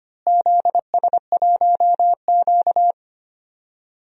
Morse code, heard as a keyed tone, about 25 wpm, 700 Hz. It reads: ZS1Q
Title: DXCC-100 Call Signs - 4 Characters - Repeated 2x - 25wpm